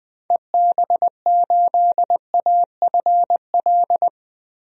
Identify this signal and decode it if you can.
Morse code, heard as a keyed tone, about 20 wpm, 700 Hz. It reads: EB8AFL
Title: DXCC-100 Call Signs - 6 Characters - Repeated 2x - 20wpm